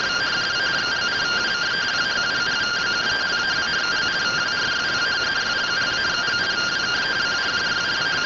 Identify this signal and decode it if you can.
Two-tone teleprinter signal, about 170 Hz shift.
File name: findings / 100 surview 0.gen-17 rtty rx